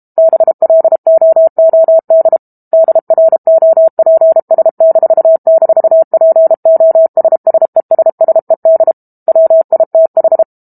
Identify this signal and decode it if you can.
Morse code, heard as a keyed tone, about 32 wpm, 650 Hz. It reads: BLOOD DROPS--POSSESSED WITH
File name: Chapter 6 - 32 WPM